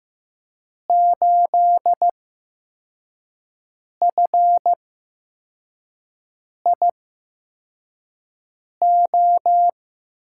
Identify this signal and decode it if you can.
Morse code, heard as a keyed tone, about 15 wpm, spaced as for 4 wpm, 700 Hz. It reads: 8FIO
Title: USA Call Signs - 6 Characters - 8x Character Spacing - 15wpm